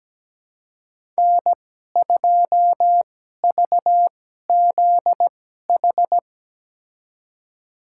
Morse code, heard as a keyed tone, about 17 wpm, spaced as for 12 wpm, 700 Hz. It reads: N2VZH
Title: USA Call Signs - 5 Characters - 2x Character Spacing - 17wpm